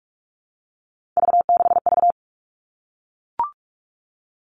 Morse code, heard as a keyed tone, about 44 wpm, 700 Hz. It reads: V6V
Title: All Cut Numbers - 3-Digit 45wpm